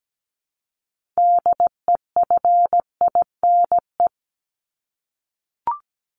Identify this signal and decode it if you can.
Morse code, heard as a keyed tone, about 17 wpm, 700 Hz. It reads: DEFINE
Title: ED Suffix - Top 100 Words with Base Words 17wpm